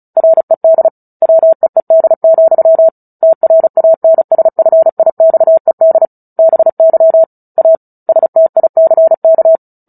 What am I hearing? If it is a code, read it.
Morse code, heard as a keyed tone, about 34 wpm, 650 Hz. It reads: RED WEED, TRANSFIXED BY A STICK
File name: Book Two / 09-Chapter 9 - 35 WPM